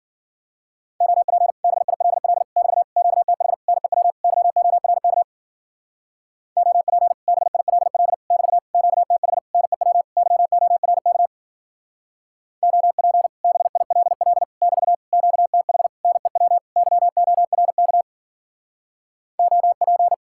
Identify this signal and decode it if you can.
Morse code, heard as a keyed tone, about 47 wpm, 700 Hz. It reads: OP BILL <BT> QTH NEW YORK OP BILL <BT> QTH NEW YORK OP BILL <BT> QTH NEW YORK OP
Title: Speed-Racing Common QSO Phrases Encore - 40wpm